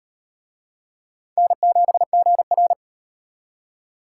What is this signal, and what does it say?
Morse code, heard as a keyed tone, about 38 wpm, 700 Hz. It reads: N7GR